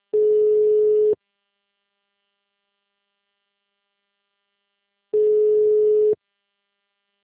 Ringback tone (Europe, 425 Hz).